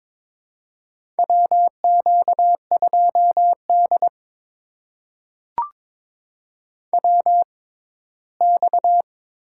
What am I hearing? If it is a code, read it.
Morse code, heard as a keyed tone, about 22 wpm, spaced as for 15 wpm, 700 Hz. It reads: WQ2D WX
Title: USA Call Signs - 4 Characters - 6x Character Spacing - 22wpm